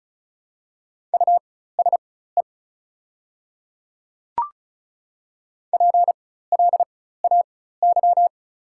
Morse code, heard as a keyed tone, about 34 wpm, spaced as for 16 wpm, 700 Hz. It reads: USE PLAY
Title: 4x Character Spacing - Top 300 Words - 35wpm